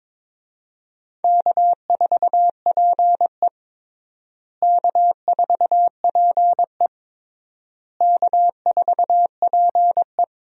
Morse code, heard as a keyed tone, about 22 wpm, 700 Hz. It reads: K4PE K4PE K4PE
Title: USA Call Signs - 4 Characters - Repeated 3x - 22wpm